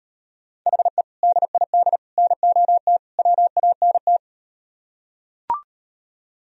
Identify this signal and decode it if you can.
Morse code, heard as a keyed tone, about 38 wpm, 700 Hz. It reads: HE DID NOT WANT